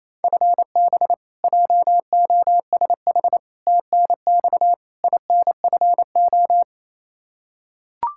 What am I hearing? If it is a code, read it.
Morse code, heard as a keyed tone, about 28 wpm, 700 Hz. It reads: FB JOSH TNX INFO
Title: Practice QSO - Exchange 3 - More Info - 28wpm